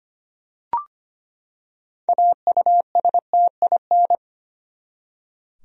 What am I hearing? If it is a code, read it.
Morse code, heard as a keyed tone, about 25 wpm, 700 Hz. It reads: AUSTIN
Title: USA Common Names - 25wpm